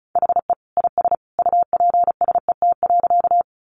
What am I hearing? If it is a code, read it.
Morse code, heard as a keyed tone, about 34 wpm, 700 Hz. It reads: HE IS UPSET.
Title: Additional Easy Phrases 35wpm